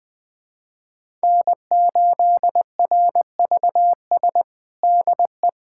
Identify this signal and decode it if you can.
Morse code, heard as a keyed tone, about 20 wpm, 700 Hz. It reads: N8RVS DE
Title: USA Callsign DE Callsign - 5 Characters - 20wpm